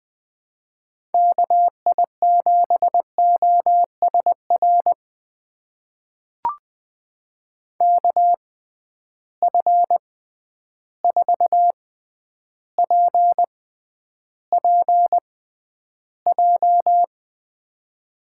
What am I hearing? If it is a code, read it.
Morse code, heard as a keyed tone, about 20 wpm, spaced as for 10 wpm, 700 Hz. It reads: KI7OSR KF4PPJ
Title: USA Call Signs - 6 Characters - 6x Character Spacing - 20wpm